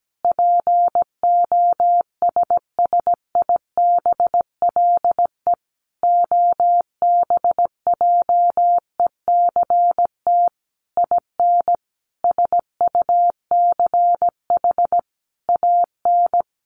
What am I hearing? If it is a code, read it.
Morse code, heard as a keyed tone, about 17 wpm, 700 Hz. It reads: POSSIBLE OBJECT IN SUCH AN
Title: Sentences from Top 1500 Words 17wpm